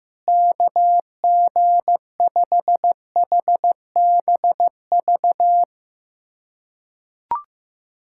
Morse code, heard as a keyed tone, about 15 wpm, 700 Hz. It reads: KG5HBV
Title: USA Callsign DE Callsign 15wpm